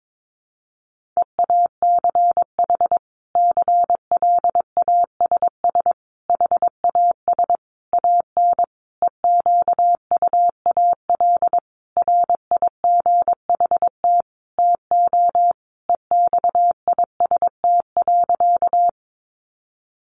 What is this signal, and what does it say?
Morse code, heard as a keyed tone, about 22 wpm, 700 Hz. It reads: EACH CLASS HAS AN EQUAL RIGHT TO EXIST.
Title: Sentences from Top 1400 Words 22wpm